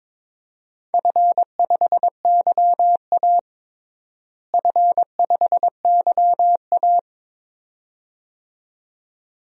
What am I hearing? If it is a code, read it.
Morse code, heard as a keyed tone, about 21 wpm, 700 Hz. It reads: F5YA F5YA
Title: DXCC-100 Call Signs - 4 Characters - Repeated 2x - 22wpm